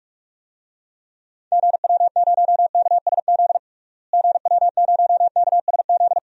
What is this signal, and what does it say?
Morse code, heard as a keyed tone, about 41 wpm, 700 Hz. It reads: GW0KSZ GW0KSZ
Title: DXCC-100 Call Signs - 6 Characters - Repeated 3x - 45wpm